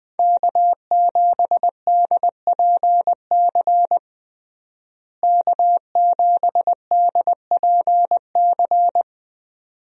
Morse code, heard as a keyed tone, about 20 wpm, 700 Hz. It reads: K7DPC K7DPC
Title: USA Call Signs - 5 Characters - Repeated 2x - 20wpm